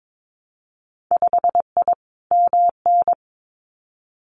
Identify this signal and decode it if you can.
Morse code, heard as a keyed tone, about 22 wpm, 700 Hz. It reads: 5I MN
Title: Winter Field Day Exchanges Encore 22wpm